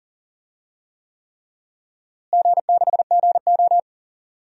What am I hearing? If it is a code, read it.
Morse code, heard as a keyed tone, about 40 wpm, 700 Hz. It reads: G6GO
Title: DXCC-100 Call Signs - 4 Characters - Repeated 3x - 40wpm